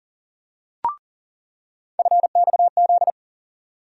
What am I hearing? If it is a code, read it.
Morse code, heard as a keyed tone, about 40 wpm, 700 Hz. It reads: FXZ
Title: USA Call Sign Postfixes - 3 Letters 40wpm